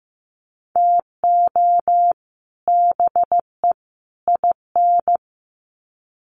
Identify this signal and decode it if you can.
Morse code, heard as a keyed tone, about 15 wpm, 700 Hz. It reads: TO BE IN
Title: Sets of 3 Words 15wpm